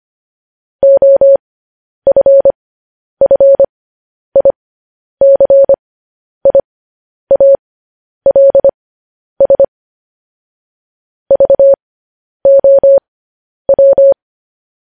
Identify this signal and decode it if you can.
Morse code, heard as a keyed tone, about 25 wpm, spaced as for 10 wpm, 550 Hz. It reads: OFFICIALS VOW
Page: news headlines in morse code at 10 wpm